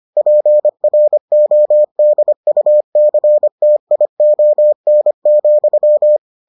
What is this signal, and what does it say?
Morse code, heard as a keyed tone, about 25 wpm, 600 Hz. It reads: PRODUCTION,